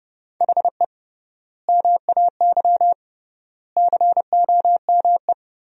Morse code, heard as a keyed tone, about 29 wpm, 700 Hz. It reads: HE MAY COME
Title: 3x Word-Spacing - Sentences from Top 300 Words - 30wpm